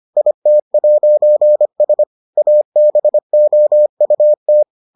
Morse code, heard as a keyed tone, about 25 wpm, 600 Hz. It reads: IT'S ABOUT